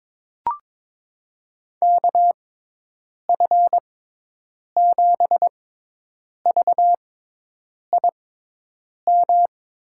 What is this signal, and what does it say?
Morse code, heard as a keyed tone, about 21 wpm, spaced as for 8 wpm, 700 Hz. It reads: KF7VIM
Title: USA Call Signs - 6 Characters - 6x Character Spacing - 22wpm